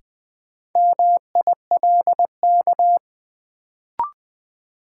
Morse code, heard as a keyed tone, about 20 wpm, 700 Hz. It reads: MILK
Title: Sentences from Top 1500 Words - New Word Drill - 20wpm